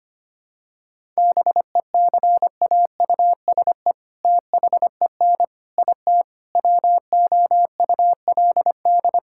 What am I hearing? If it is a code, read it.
Morse code, heard as a keyed tone, about 25 wpm, 700 Hz. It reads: BECAUSE THEN IT WOULD
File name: practice / 25wpm / All-Age Appropriate Jokes - 25wpm